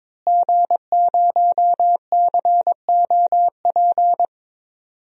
Morse code, heard as a keyed tone, about 22 wpm, 700 Hz. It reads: G0COP
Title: DXCC-100 Call Signs - 5 Characters - Repeated 3x - 22wpm